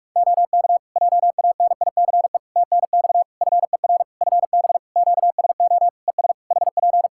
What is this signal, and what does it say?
Morse code, heard as a keyed tone, about 46 wpm, 700 Hz. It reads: OK JANICE TNX FER FB QSO ES HP
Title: K1USN SST QSO Practice 45wpm